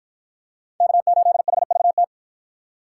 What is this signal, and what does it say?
Morse code, heard as a keyed tone, about 53 wpm, 700 Hz. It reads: K8HVT